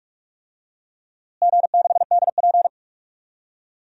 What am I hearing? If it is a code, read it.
Morse code, heard as a keyed tone, about 46 wpm, 700 Hz. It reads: G6DP
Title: DXCC-100 Call Signs - 4 Characters - Repeated 2x - 45wpm